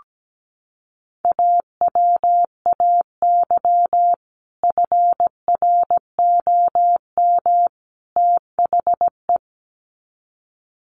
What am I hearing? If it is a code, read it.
Morse code, heard as a keyed tone, about 17 wpm, 700 Hz. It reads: AWAY FROM THE
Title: Sets of 3 Words 17wpm